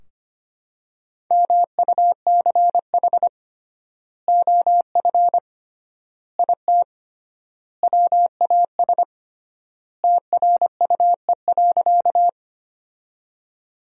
Morse code, heard as a keyed tone, about 25 wpm, 700 Hz. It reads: MUCH OF IT WAS TRUE.
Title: 3x Word-Spacing - Sentences from Top 400 Words - 25wpm